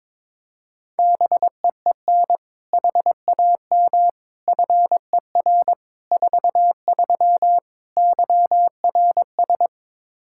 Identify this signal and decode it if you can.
Morse code, heard as a keyed tone, about 22 wpm, 700 Hz. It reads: BEEN HAM FER 43 YRS
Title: Practice QSO - Exchange 3 - More Info - 22wpm